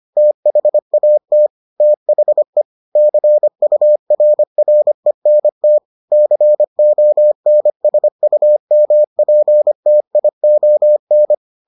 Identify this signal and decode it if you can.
Morse code, heard as a keyed tone, about 25 wpm, 600 Hz. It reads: THAT THE CURRENT CONSUMPTION